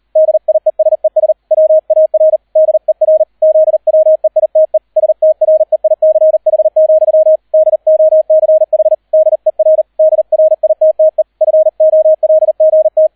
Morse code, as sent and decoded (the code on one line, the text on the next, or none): -.. .. ... ... / .-- .- .-. / -.. .-. / --.. .-- . .. - . / ... - .-. .. -.-. .... --..-- / -.. --- -.-. .... / -.. .-. / -.. .-. .. - - . / ..-. --- .-.. --. -
DIESES WAR DER ZWEITE STREICH, DOCH DER DRITTE FOLGT